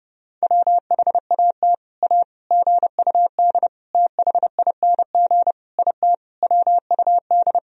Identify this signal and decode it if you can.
Morse code, heard as a keyed tone, about 30 wpm, 700 Hz. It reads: WHAT A GUD THING IT WUD
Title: Sentences from Top 200 Words and Abbreviations 30wpm